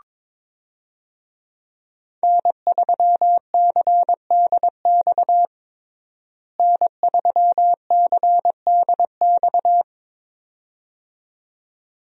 Morse code, heard as a keyed tone, about 22 wpm, 700 Hz. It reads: N3CDX N3CDX
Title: USA Call Signs - 5 Characters - Repeated 2x - 22wpm